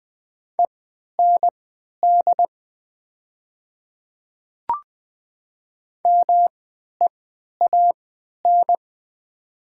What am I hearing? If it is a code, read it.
Morse code, heard as a keyed tone, about 20 wpm, spaced as for 11 wpm, 700 Hz. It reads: END MEAN
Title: 3x Character Spacing - Top 200 Words - 20wpm